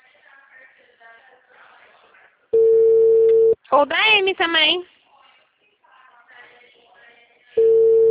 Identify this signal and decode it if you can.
Ringback tone (China, 450 Hz), with speech.
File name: out- -2027- - - .2839